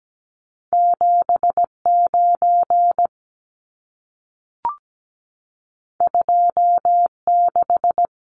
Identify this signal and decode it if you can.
Morse code, heard as a keyed tone, about 17 wpm, 700 Hz. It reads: 79 26
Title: Numbers - 2-Digit 17wpm